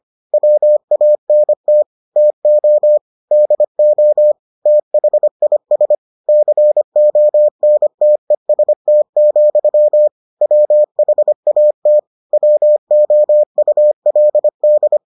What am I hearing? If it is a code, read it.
Morse code, heard as a keyed tone, about 25 wpm, 600 Hz. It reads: WANT TO DO THIS CONTEST, WHAT WOULD